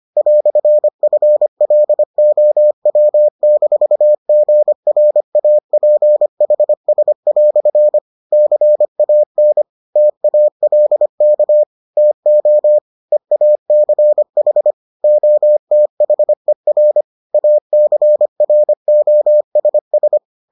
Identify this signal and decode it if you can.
Morse code, heard as a keyed tone, about 25 wpm, 600 Hz. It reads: "FLOW-GRAPHS" CAN TALK TO EACH OTHER ACROSS